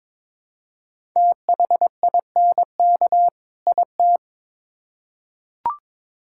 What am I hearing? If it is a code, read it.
Morse code, heard as a keyed tone, about 22 wpm, 700 Hz. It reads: THINK IT
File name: Build A Phrase - 2-Word Phrases - 22wpm